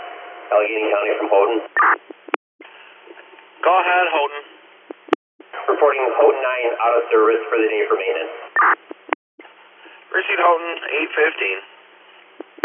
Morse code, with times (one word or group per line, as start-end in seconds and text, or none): none